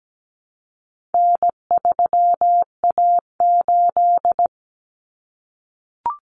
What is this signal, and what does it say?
Morse code, heard as a keyed tone, about 17 wpm, 700 Hz. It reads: N3A8
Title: Common Cut Numbers - 4-Digit 17wpm